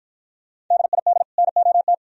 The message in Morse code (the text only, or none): DID NOT